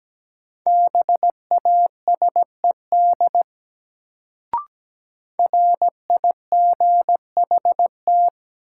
Morse code, heard as a keyed tone, about 17 wpm, 700 Hz. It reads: BASED RIGHT
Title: Sets of 2 Words - New Word Drill - 17wpm